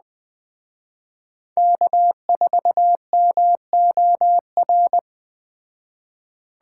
Morse code, heard as a keyed tone, about 20 wpm, 700 Hz. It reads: K4MOR
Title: Speed Racing - USA Call Signs 17wpm